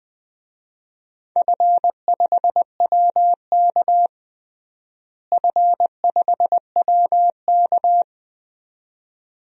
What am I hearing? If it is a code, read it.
Morse code, heard as a keyed tone, about 20 wpm, 700 Hz. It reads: F5WK F5WK
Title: DXCC-100 Call Signs - 4 Characters - Repeated 2x - 20wpm